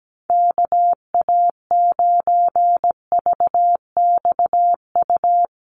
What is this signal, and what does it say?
Morse code, heard as a keyed tone, about 17 wpm, 700 Hz. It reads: KA9VXU
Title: USA Call Signs Repeated 2x 17wpm